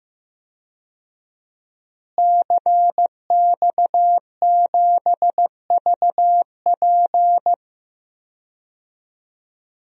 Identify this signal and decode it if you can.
Morse code, heard as a keyed tone, about 15 wpm, 700 Hz. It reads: CX7VP
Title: DXCC-100 Call Signs - 5 Characters - Repeated 2x - 15wpm